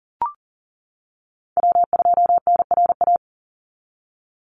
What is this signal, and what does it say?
Morse code, heard as a keyed tone, about 40 wpm, 700 Hz. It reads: W2NRA
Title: Call Signs - 40wpm